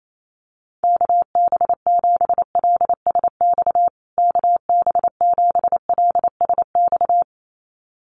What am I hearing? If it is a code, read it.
Morse code, heard as a keyed tone, about 28 wpm, 700 Hz. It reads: KB7LSX KB7LSX
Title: USA Call Signs Repeated 2x 28wpm